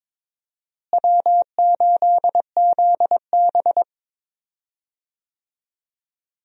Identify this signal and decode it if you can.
Morse code, heard as a keyed tone, about 22 wpm, 700 Hz. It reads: W8ZB
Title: USA Call Signs - 4 Characters - Repeated 3x - 22wpm